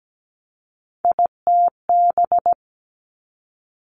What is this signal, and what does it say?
Morse code, heard as a keyed tone, about 17 wpm, 700 Hz. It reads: ITB
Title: Three Challenging Letter-Number Combinations 17wpm